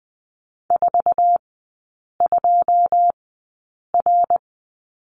Morse code, heard as a keyed tone, about 20 wpm, spaced as for 8 wpm, 700 Hz. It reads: 42R
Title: 107 Lesson 18e - Letter m - Three Characters 20wpm